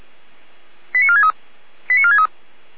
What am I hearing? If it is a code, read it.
Five-tone selective call (ZVEI), heard as quick runs of five tones, tones of about 70 ms. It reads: 89463 89463